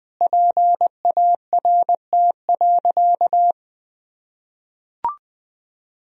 Morse code, heard as a keyed tone, about 21 wpm, 700 Hz. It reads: PART.
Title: Sentences from Top 1000 Words - Review Part 3 - 20wpm